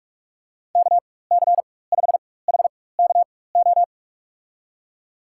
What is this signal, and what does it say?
Morse code, heard as a keyed tone, about 46 wpm, spaced as for 21 wpm, 700 Hz. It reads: KC5HKO
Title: USA Call Signs - 6 Characters - 4x Character Spacing - 45wpm